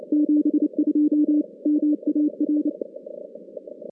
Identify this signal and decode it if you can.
Morse code, heard as a keyed tone, about 29 wpm, 300 Hz. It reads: 72 MAR
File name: contests / CW-SS-k5tr- -2100